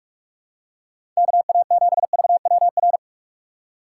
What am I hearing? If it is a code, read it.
Morse code, heard as a keyed tone, about 46 wpm, 700 Hz. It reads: KA7VWR
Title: USA Call Signs - 6 Characters - 3x Character Spacing - 45wpm